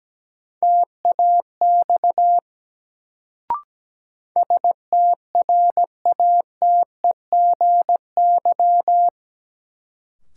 Morse code, heard as a keyed tone, about 17 wpm, 700 Hz. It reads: TAX STRATEGY
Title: Sentences from Top 1000 Words - Review Part 5 - New Word Drill - 17wpm